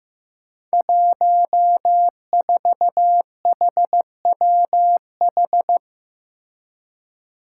Morse code, heard as a keyed tone, about 15 wpm, 700 Hz. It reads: 14HWH
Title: Five Challenging Letter-Number Combinations 15wpm